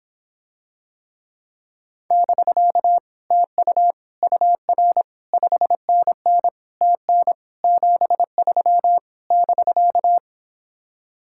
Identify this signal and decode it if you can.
Morse code, heard as a keyed tone, about 26 wpm, 700 Hz. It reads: <BK> TU UR 5NN TN 73 <BK>